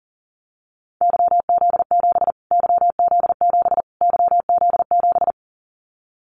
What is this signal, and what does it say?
Morse code, heard as a keyed tone, about 40 wpm, 700 Hz. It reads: YZ7 YZ7 YZ7
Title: Callsign Prefixes for Top 25 DXCC - Repeated 3x - 40wpm